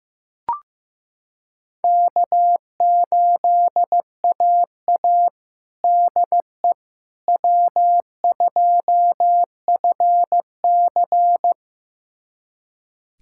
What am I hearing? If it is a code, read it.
Morse code, heard as a keyed tone, about 15 wpm, 700 Hz. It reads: K8AA DE W2FC
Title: USA Callsign DE Callsign - 4 Characters - 15wpm